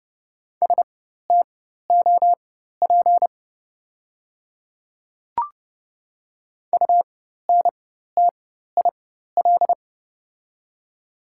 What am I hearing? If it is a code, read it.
Morse code, heard as a keyed tone, about 30 wpm, spaced as for 14 wpm, 690 Hz. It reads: STOP UNTIL
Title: 4x Character Spacing - Top 300 Words - 30wpm